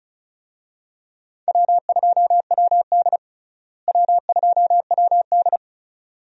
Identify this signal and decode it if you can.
Morse code, heard as a keyed tone, about 36 wpm, 700 Hz. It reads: W2WD W2WD
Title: USA Call Signs - 4 Characters - Repeated 2x - 35wpm